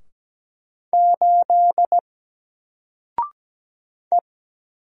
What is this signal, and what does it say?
Morse code, heard as a keyed tone, about 17 wpm, 700 Hz. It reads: 8 E
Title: Single Character (Letter:Number:Punctuation:Prosign) 17wpm